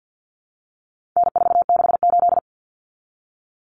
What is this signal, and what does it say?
Morse code, heard as a keyed tone, about 50 wpm, 700 Hz. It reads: N468
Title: Common Cut Numbers - 4-Digit 50wpm